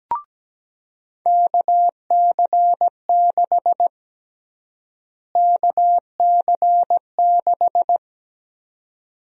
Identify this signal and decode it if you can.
Morse code, heard as a keyed tone, about 17 wpm, 700 Hz. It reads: KC6 KC6
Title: USA Call Sign Prefixes - Repeated 2x - 17wpm